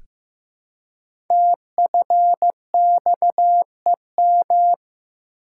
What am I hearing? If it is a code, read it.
Morse code, heard as a keyed tone, about 15 wpm, 700 Hz. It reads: TFXEM
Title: Code Groups - 5 Letters - 15wpm